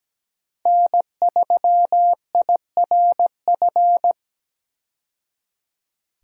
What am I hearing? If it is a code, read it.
Morse code, heard as a keyed tone, about 17 wpm, 700 Hz. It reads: N3IRF
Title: USA Call Signs - 5 Characters - 17wpm